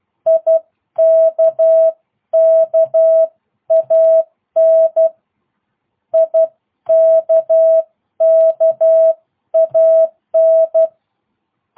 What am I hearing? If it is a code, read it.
Morse code, heard as a keyed tone, about 12 wpm, 650 Hz. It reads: IKKAN IKKAN